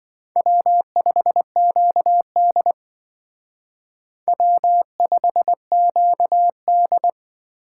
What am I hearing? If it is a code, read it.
Morse code, heard as a keyed tone, about 23 wpm, 700 Hz. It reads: W5QD W5QD
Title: Speed Racing - USA 1x2 Call Signs 20wpm